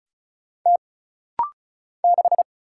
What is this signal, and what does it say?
Morse code, heard as a keyed tone, about 36 wpm, 700 Hz. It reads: T 6
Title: Single Letter-Number - Rapid-Fire 35wpm